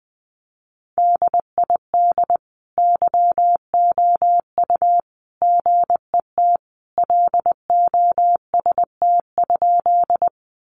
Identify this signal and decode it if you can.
Morse code, heard as a keyed tone, about 20 wpm, 700 Hz. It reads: DID YOU GET LOST?
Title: Sentences from Top 1400 Words 20wpm